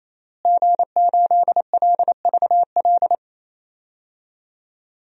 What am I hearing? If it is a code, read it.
Morse code, heard as a keyed tone, about 28 wpm, 700 Hz. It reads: G8LVL
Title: DXCC-100 Call Signs - 5 Characters - Repeated 3x - 28wpm